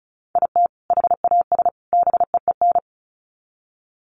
Morse code, heard as a keyed tone, about 34 wpm, 700 Hz. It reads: IT HAS BEEN
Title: Sets of 3 Words 35wpm